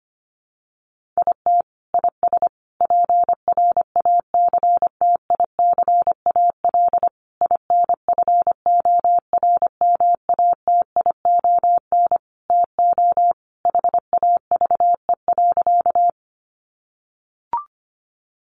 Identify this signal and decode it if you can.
Morse code, heard as a keyed tone, about 25 wpm, 700 Hz. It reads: IT IS PRACTICAL INFORMATION TO HAVE.
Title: Sentences from Top 1400 Words 25wpm